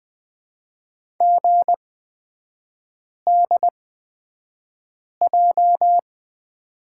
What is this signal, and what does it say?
Morse code, heard as a keyed tone, about 20 wpm, spaced as for 5 wpm, 690 Hz. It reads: GDJ
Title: 229 Lesson 37e - Letter x - Three Characters 20wpm